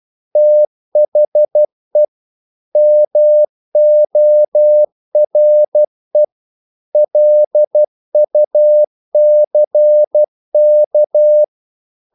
Morse code, heard as a keyed tone, about 12 wpm, 600 Hz. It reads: THE MORE LUCK
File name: CW / Practice / EnglishQuotations0112